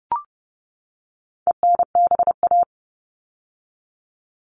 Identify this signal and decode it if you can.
Morse code, heard as a keyed tone, about 30 wpm, 700 Hz. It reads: ENBA
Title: All Cut Numbers - 4-Digit 30wpm